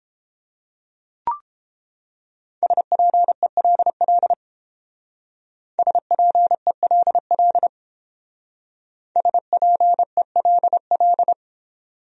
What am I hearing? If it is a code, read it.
Morse code, heard as a keyed tone, about 29 wpm, 700 Hz. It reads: SPELL SPELL SPELL